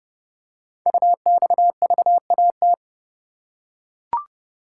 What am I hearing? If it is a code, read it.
Morse code, heard as a keyed tone, about 30 wpm, 700 Hz. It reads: UXVAT
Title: Code Groups - 5 Letters - 30wpm